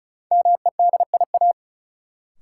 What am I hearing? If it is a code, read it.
Morse code, heard as a keyed tone, about 34 wpm, 700 Hz. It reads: MEDIA
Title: Sentences from Top 1000 Words - Review Part 4 - New Word Drill - 35wpm